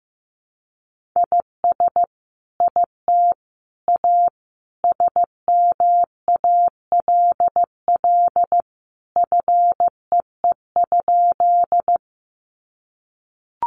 Morse code, heard as a keyed tone, about 15 wpm, 700 Hz. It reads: IS IT A SMALL FEE?
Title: Sentences from Top 1500 Words 15wpm